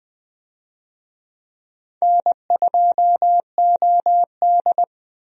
Morse code, heard as a keyed tone, about 20 wpm, 700 Hz. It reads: N2OD
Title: USA Call Signs - 4 Characters - 20wpm